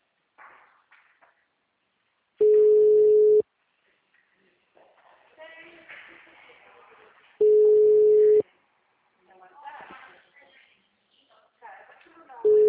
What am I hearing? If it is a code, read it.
Ringback tone (Europe, 425 Hz), with speech.